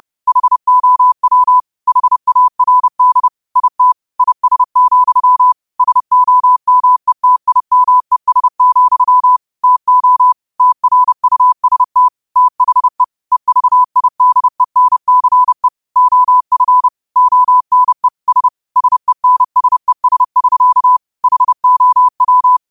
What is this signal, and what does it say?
Morse code, heard as a keyed tone, about 30 wpm, 1000 Hz. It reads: HOW HARD IT IS, SOMETIMES, TO TRUST THE EVIDENCE OF ONE S SENSES <SK> HOW